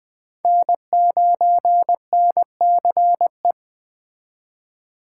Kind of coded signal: Morse code

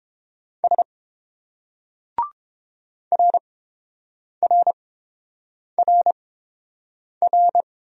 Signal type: Morse code

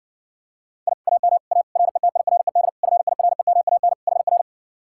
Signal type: Morse code

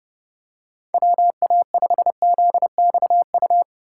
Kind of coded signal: Morse code

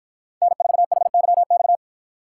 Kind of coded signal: Morse code

